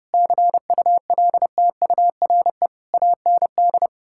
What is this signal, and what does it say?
Morse code, heard as a keyed tone, about 30 wpm, 700 Hz. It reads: CULTURE AND
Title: Sentences from Top 1000 Words - Review Part 6 - 30wpm